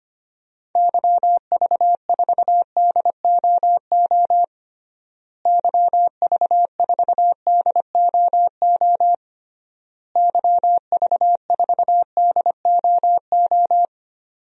Morse code, heard as a keyed tone, about 25 wpm, 700 Hz. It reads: YV4DOO YV4DOO YV4DOO
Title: DXCC-100 Call Signs - 6 Characters - Repeated 3x - 25wpm